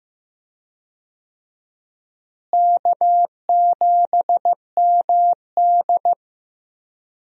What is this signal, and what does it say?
Morse code, heard as a keyed tone, about 15 wpm, 700 Hz. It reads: K7MD
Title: USA Call Signs - 4 Characters - Repeated 2x - 15wpm